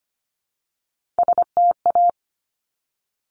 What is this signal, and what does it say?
Morse code, heard as a keyed tone, about 25 wpm, 700 Hz. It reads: STA
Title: Top Three Letter Combinations in Words 25wpm